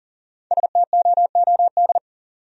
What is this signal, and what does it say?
Morse code, heard as a keyed tone, about 54 wpm, 700 Hz. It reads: STOOD